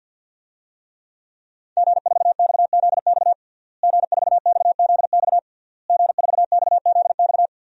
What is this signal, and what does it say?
Morse code, heard as a keyed tone, about 50 wpm, 700 Hz. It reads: G4XZX G4XZX G4XZX
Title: DXCC-100 Call Signs - 5 Characters - Repeated 3x - 50wpm